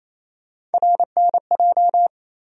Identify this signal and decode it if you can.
Morse code, heard as a keyed tone, about 28 wpm, 700 Hz. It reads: RNJ
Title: USA Call Sign Postfixes - 3 Letters 28wpm